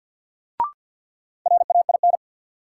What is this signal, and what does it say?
Morse code, heard as a keyed tone, about 53 wpm, 700 Hz. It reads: RAIN
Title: Sentences from Top 600 Words - New Word Drill - 50wpm